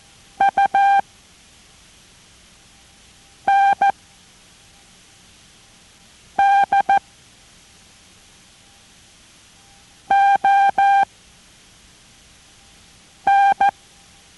Morse code, heard as a keyed tone, about 14 wpm, 790 Hz. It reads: U N D O N